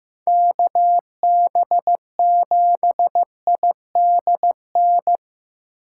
Morse code, heard as a keyed tone, about 15 wpm, 700 Hz. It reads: KB7ID N
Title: USA Call Signs - 6x Character Spacing - 15wpm